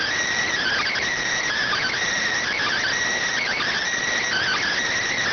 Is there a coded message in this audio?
Yes: Emergency Alert System (SAME) tones from 0.5 to 4.7 s, heard as harsh buzzing bursts.